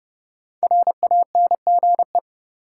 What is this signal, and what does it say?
Morse code, heard as a keyed tone, about 30 wpm, 700 Hz. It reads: RANGE
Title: Sets of 4 Words - New Word Drill - 30wpm